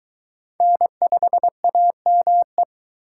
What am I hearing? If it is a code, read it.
Morse code, heard as a keyed tone, about 22 wpm, 700 Hz. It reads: N5AME